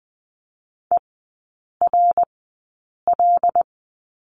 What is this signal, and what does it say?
Morse code, heard as a keyed tone, about 20 wpm, spaced as for 8 wpm, 690 Hz. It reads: ERL
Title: 133 Lesson 22e - Question Mark - Three Characters 20wpm